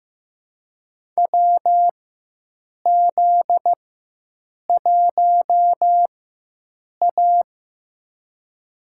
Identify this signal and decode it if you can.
Morse code, heard as a keyed tone, about 15 wpm, spaced as for 7 wpm, 700 Hz. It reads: WZ1A
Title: USA Call Signs - 4 Characters - 4x Character Spacing - 15wpm